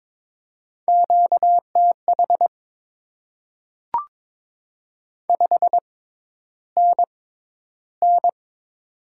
Morse code, heard as a keyed tone, about 22 wpm, spaced as for 11 wpm, 690 Hz. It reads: QTH 5NN
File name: Top 20 QSO Elements - 6x Character Spacing - 22wpm